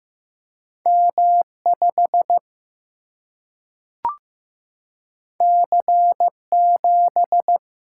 Morse code, heard as a keyed tone, about 15 wpm, 700 Hz. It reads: M5 C7
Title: Single-Letter Single-Number 15wpm